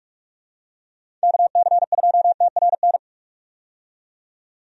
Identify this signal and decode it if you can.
Morse code, heard as a keyed tone, about 47 wpm, 700 Hz. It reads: KC2TRN
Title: USA Call Signs - 6 Characters - 45wpm